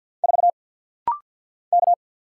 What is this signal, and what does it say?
Morse code, heard as a keyed tone, about 50 wpm, 690 Hz. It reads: 4 K
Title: Single Letter-Number - Rapid-Fire 50wpm